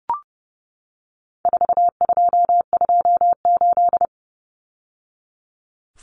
Morse code, heard as a keyed tone, about 29 wpm, 700 Hz. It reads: 4228
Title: Numbers - 4-Digit 30wpm